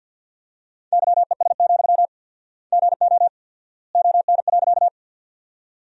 Morse code, heard as a keyed tone, about 53 wpm, 700 Hz. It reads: YES, GO ON.
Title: 4x Word-Spacing - Sentences from Top 500 Words - 50wpm